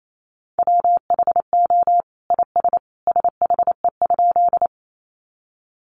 Morse code, heard as a keyed tone, about 28 wpm, 700 Hz. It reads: WHO IS SHE?
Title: Sentences from Top 100 Words 28wpm